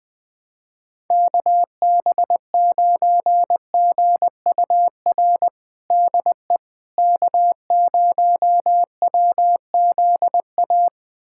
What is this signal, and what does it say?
Morse code, heard as a keyed tone, about 20 wpm, 700 Hz. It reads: KB9GUR DE K0WZA
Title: USA Callsign DE Callsign 20wpm